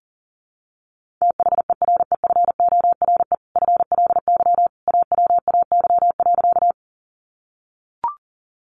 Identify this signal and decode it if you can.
Morse code, heard as a keyed tone, about 40 wpm, 700 Hz. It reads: THEREFORE FLY AWAY.
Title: Sentences from Top 800 Words 40wpm